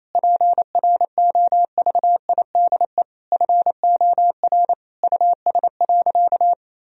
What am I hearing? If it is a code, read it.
Morse code, heard as a keyed tone, about 28 wpm, 700 Hz. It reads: PROVIDE FOR US.
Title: Sentences from Top 1000 Words - Review Part 3 - 28wpm